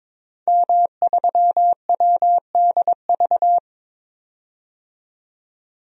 Morse code, heard as a keyed tone, about 22 wpm, 700 Hz. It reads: M3WDV
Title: DXCC-100 Call Signs - 5 Characters - 22wpm